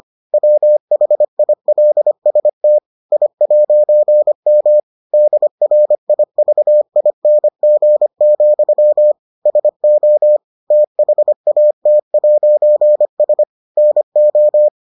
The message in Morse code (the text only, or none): WHILST I'M DRIVING, SO THAT'S NO